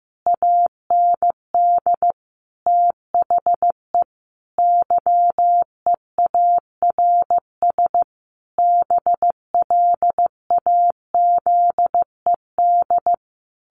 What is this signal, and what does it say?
Morse code, heard as a keyed tone, about 15 wpm, 700 Hz. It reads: AND THE YEARS BLAZED
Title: A Sound Of Thunder 15wpm